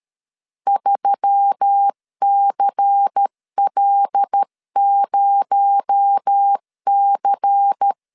Morse code, heard as a keyed tone, about 13 wpm, 800 Hz. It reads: 3CL0C
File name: CW / DET / 1 6 1500C S C 12wpm 3533